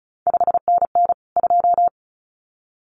Morse code, heard as a keyed tone, about 36 wpm, 700 Hz. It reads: 5NN 2
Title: CQ WW DX Contest Exchange - 35wpm